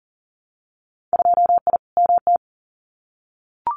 Morse code, heard as a keyed tone, about 40 wpm, 700 Hz. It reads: 2I MT
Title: Winter Field Day Exchanges Encore 40wpm